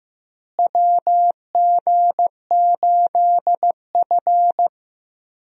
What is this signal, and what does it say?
Morse code, heard as a keyed tone, about 16 wpm, 700 Hz. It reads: WG8F
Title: USA Call Signs - 4 Characters - Repeated 2x - 15wpm